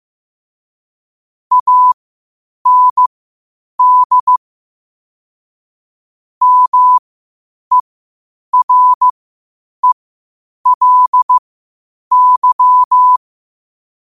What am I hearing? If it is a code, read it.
Morse code, heard as a keyed tone, about 15 wpm, spaced as for 8 wpm, 1000 Hz. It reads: AND MERELY
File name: qotdfiles / 8wpm